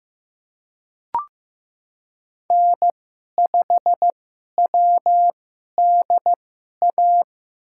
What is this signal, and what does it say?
Morse code, heard as a keyed tone, about 15 wpm, spaced as for 11 wpm, 700 Hz. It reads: N5WDA
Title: USA Call Signs - 5 Characters - 2x Character Spacing - 15wpm